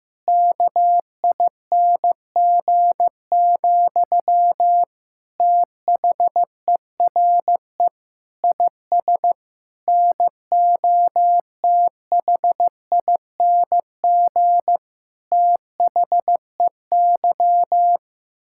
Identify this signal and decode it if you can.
Morse code, heard as a keyed tone, about 15 wpm, 700 Hz. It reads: KING, THERE IS NOTHING THEY
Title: Sentences from Top 1000 Words - Review Part 6 - 15wpm